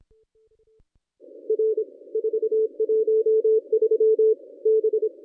Morse code, heard as a keyed tone, about 26 wpm, 440 Hz. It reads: R 413 B
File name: contests / ss-cw-k5tr-op-n5rz -1200